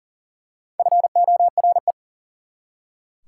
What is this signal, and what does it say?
Morse code, heard as a keyed tone, about 40 wpm, 700 Hz. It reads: FORE
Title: Prefixes and Suffixes 40wpm